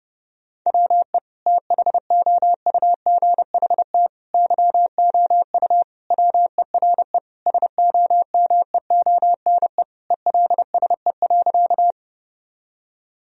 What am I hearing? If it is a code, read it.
Morse code, heard as a keyed tone, about 30 wpm, 700 Hz. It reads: WE THOUGHT YOU WERE SOMEONE ELSE.
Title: Sentences from Top 1000 Words - Review Part 4 - 30wpm